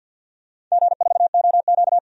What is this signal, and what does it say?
Morse code, heard as a keyed tone, about 52 wpm, 700 Hz. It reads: G4OQ
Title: DXCC-100 Call Signs - 4 Characters - 50wpm